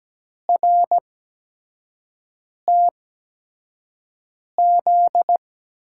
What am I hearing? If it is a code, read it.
Morse code, heard as a keyed tone, about 17 wpm, spaced as for 5 wpm, 700 Hz. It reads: RTZ
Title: USA Call Signs - 5 Characters - 8x Character Spacing - 17wpm